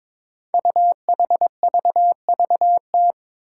Morse code, heard as a keyed tone, about 22 wpm, 700 Hz. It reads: UHVVT
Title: Five Challenging Letter-Number Combinations 22wpm